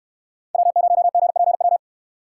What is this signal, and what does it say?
Morse code, heard as a keyed tone, about 69 wpm, 700 Hz. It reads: W0GPW